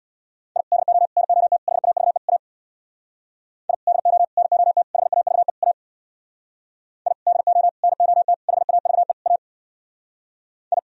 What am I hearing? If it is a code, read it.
Morse code, heard as a keyed tone, about 64 wpm, 700 Hz. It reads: I DO NOT HAVE A I DO NOT HAVE A I DO NOT HAVE A I